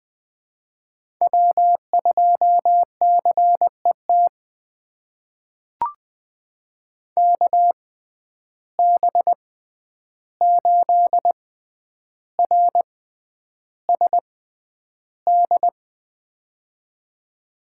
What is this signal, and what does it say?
Morse code, heard as a keyed tone, about 21 wpm, spaced as for 10 wpm, 700 Hz. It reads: W2CET KB8RSD
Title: USA Call Signs - 6x Character Spacing - 20wpm